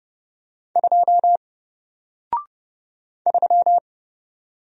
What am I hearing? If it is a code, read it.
Morse code, heard as a keyed tone, about 30 wpm, 700 Hz. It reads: 2 3
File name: Single-Letter-Number-30wpm